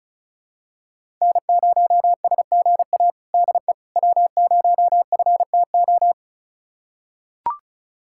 Morse code, heard as a keyed tone, about 34 wpm, 700 Hz. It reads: N0SGA DE W0FTO
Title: USA Callsign DE Callsign - 5 Characters - 35wpm